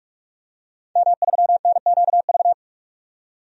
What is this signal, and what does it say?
Morse code, heard as a keyed tone, about 47 wpm, 700 Hz. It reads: M3NQV
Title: DXCC-100 Call Signs - 5 Characters - 45wpm